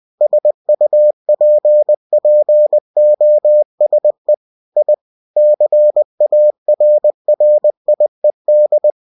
Morse code, heard as a keyed tone, about 20 wpm, 600 Hz. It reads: SUPPOSE I CARRIED
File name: cw-books / winnie-the-pooh / winnie-20wpm-page-0060